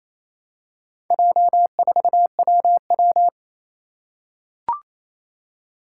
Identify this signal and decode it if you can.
Morse code, heard as a keyed tone, about 28 wpm, 700 Hz. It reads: J4WW
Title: Four Challenging Letter-Number Combinations 28wpm